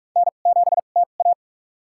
Morse code, heard as a keyed tone, about 46 wpm, spaced as for 33 wpm, 700 Hz. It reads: N7TA
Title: USA Call Signs - 4 Characters - 2x Character Spacing - 45wpm